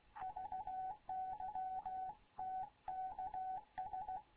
Morse code, heard as a keyed tone, about 16 wpm, 750 Hz. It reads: VY TKS
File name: Morze-HA2MN / hangp / 2 11 GYAKORLAT